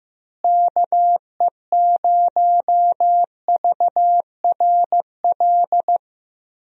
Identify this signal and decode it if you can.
Morse code, heard as a keyed tone, about 15 wpm, 700 Hz. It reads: KE0VRL
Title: USA Call Signs - 6 Characters - Repeated 3x - 15wpm